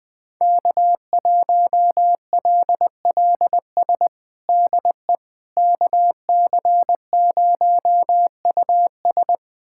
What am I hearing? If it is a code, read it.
Morse code, heard as a keyed tone, about 21 wpm, 700 Hz. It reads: K1LLS DE KC0US
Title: USA Callsign DE Callsign - 5 Characters - 20wpm